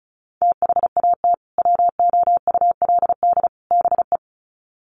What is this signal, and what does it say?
Morse code, heard as a keyed tone, about 34 wpm, 700 Hz. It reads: THAT WOULD BE
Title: Sets of 3 Words 35wpm